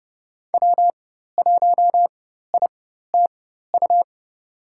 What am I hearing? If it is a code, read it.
Morse code, heard as a keyed tone, about 30 wpm, spaced as for 14 wpm, 700 Hz. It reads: W1ITU
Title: USA Call Signs - 4x Character Spacing - 30wpm